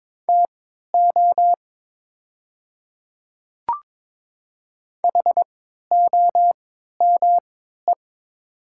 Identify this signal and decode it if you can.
Morse code, heard as a keyed tone, about 21 wpm, spaced as for 12 wpm, 700 Hz. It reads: TO HOME
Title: 3x Character Spacing - Top 500 Words - 22wpm